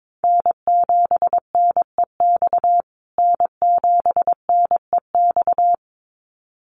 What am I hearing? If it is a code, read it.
Morse code, heard as a keyed tone, about 22 wpm, 700 Hz. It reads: N7NEX N7NEX
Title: USA Call Signs Repeated 2x 22wpm